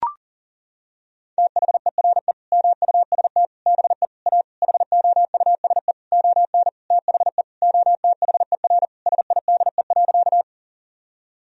Morse code, heard as a keyed tone, about 40 wpm, 700 Hz. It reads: THERE MUST BE A HOUSE ON THE OTHER SIDE.
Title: Sentences from Top 1000 Words - Review Part 1 - 40wpm